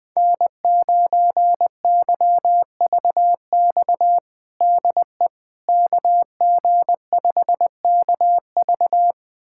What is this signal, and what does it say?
Morse code, heard as a keyed tone, about 20 wpm, 700 Hz. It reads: N9YVX DE KG5KV
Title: USA Callsign DE Callsign - 5 Characters - 20wpm